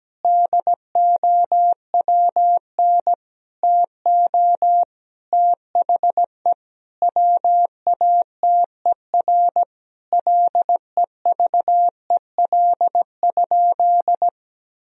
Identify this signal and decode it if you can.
Morse code, heard as a keyed tone, about 17 wpm, 700 Hz. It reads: DOWN TO THE WATER LEVEL?
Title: Sentences from Top 1000 Words - Review Part 4 - 17wpm